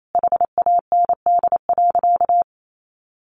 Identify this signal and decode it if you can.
Morse code, heard as a keyed tone, about 28 wpm, 700 Hz. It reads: HAND.
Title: Sentences from Top 1400 Words 28wpm